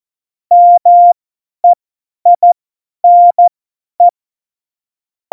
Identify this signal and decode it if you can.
Morse code, heard as a keyed tone, about 14 wpm, spaced as for 10 wpm, 700 Hz. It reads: MEINE
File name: Morsekurs / Witze / Witz 19 Mai 2020, 50 F70